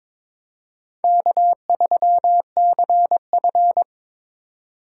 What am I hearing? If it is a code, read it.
Morse code, heard as a keyed tone, about 22 wpm, 700 Hz. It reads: K3CF
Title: USA Call Signs - 4 Characters - 3x Character Spacing - 22wpm